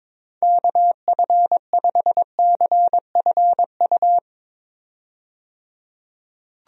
Morse code, heard as a keyed tone, about 22 wpm, 700 Hz. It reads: KF5CFU
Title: USA Call Signs - 6 Characters - 22wpm